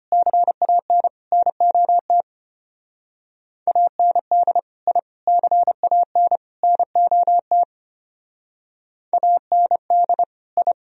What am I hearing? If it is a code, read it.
Morse code, heard as a keyed tone, about 31 wpm, 700 Hz. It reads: CAN NOT AND I CAN NOT AND I
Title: Speed-Racing - Sets of 4 Words 25wpm